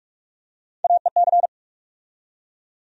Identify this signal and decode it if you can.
Morse code, heard as a keyed tone, about 43 wpm, 700 Hz. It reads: AEC
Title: USA Call Sign Postfixes - 3 Letters 45wpm